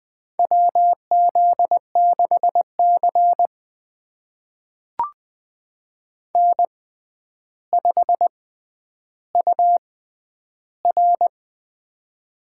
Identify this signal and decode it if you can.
Morse code, heard as a keyed tone, about 20 wpm, spaced as for 10 wpm, 700 Hz. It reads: WZ6C N5UR
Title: USA Call Signs - 4 Characters - 6x Character Spacing - 20wpm